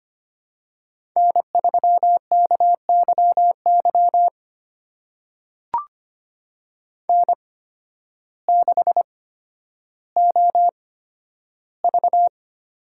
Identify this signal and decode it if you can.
Morse code, heard as a keyed tone, about 25 wpm, spaced as for 12 wpm, 700 Hz. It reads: N3KYY N6OV
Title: USA Call Signs - 5 Characters - 8x Character Spacing - 25wpm